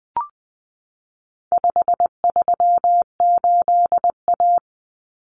Morse code, heard as a keyed tone, about 20 wpm, 700 Hz. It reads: 538A
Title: Common Cut Numbers - 4-Digit 20wpm